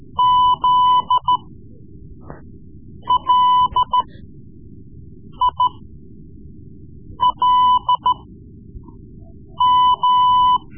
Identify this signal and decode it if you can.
Morse code, heard as a keyed tone, about 13 wpm, spaced as for 5 wpm, 1000 Hz. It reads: ZLILM
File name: Morse Lessons / Morse Lessons / Tape-3B